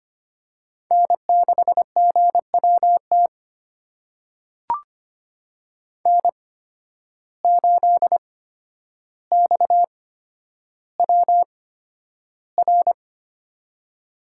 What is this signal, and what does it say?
Morse code, heard as a keyed tone, about 25 wpm, spaced as for 11 wpm, 700 Hz. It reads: N6GWT N8XWR
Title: USA Call Signs - 5 Characters - 8x Character Spacing - 25wpm